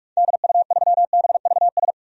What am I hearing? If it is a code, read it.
Morse code, heard as a keyed tone, about 43 wpm, 700 Hz. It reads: DU3BVS
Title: DXCC-100 Call Signs - 6 Characters - 45wpm